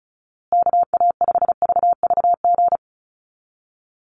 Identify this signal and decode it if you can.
Morse code, heard as a keyed tone, about 34 wpm, 700 Hz. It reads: KA5VVG
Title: USA Call Signs Repeated 3x 35wpm